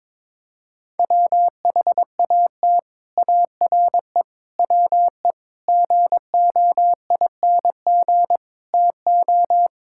Morse code, heard as a keyed tone, about 22 wpm, 700 Hz. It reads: WHAT ARE WE GOING TO
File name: practice / 22wpm / Sets of 6 Words - 22wpm